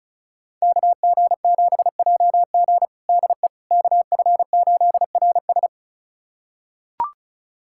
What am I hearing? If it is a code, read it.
Morse code, heard as a keyed tone, about 36 wpm, 700 Hz. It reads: KG7JG DE KF8RS
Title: USA Callsign DE Callsign - 5 Characters - 35wpm